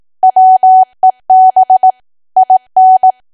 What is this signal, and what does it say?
Morse code, heard as a keyed tone, about 18 wpm, 750 Hz. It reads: WEB IN
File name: cwa / W1AW-Level2 / 18WPM